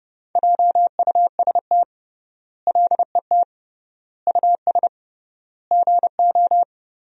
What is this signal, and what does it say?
Morse code, heard as a keyed tone, about 30 wpm, 700 Hz. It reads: JUST LET US GO
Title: 3x Word-Spacing - Sentences from Top 500 Words - 30wpm